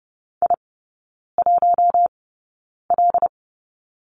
Morse code, heard as a keyed tone, about 30 wpm, spaced as for 9 wpm, 700 Hz. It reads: I1L
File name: Lesson 19e - Letter w - Three Characters 30wpm